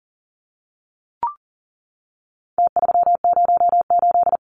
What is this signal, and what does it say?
Morse code, heard as a keyed tone, about 40 wpm, 700 Hz. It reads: T308
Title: Numbers - Contest Style - 4-Digit 40wpm